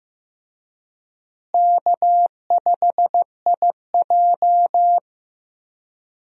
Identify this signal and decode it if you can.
Morse code, heard as a keyed tone, about 15 wpm, 700 Hz. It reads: K5IJ
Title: USA Call Signs - 4 Characters - Repeated 2x - 15wpm